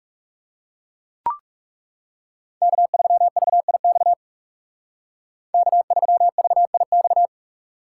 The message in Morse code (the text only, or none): K3VIX K3VIX